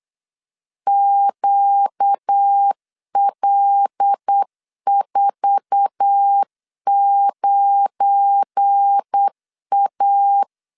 Morse code, heard as a keyed tone, about 8 wpm, 800 Hz. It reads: QL49A
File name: LEZIONI / 10 SIGNS / 10C DOT 8wpm 3649